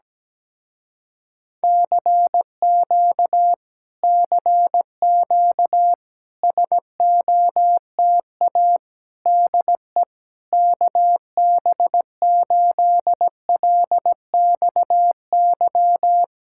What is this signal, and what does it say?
Morse code, heard as a keyed tone, about 17 wpm, 700 Hz. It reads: CQ CQ SOTA DE KB8LXY
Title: Speed-Racing Common QSO Phrases - 17wpm